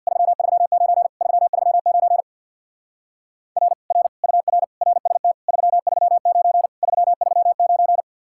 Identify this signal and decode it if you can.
Morse code, heard as a keyed tone, about 60 wpm, 700 Hz. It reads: 339 339 R R UR RST 339 339
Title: Speed-Racing Common QSO Phrases - 50wpm